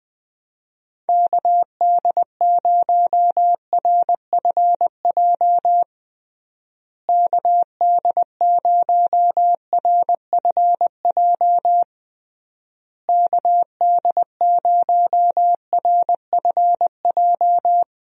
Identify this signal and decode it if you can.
Morse code, heard as a keyed tone, about 21 wpm, 700 Hz. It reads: KD0RFJ KD0RFJ KD0RFJ
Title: USA Call Signs - 6 Characters - Repeated 3x - 20wpm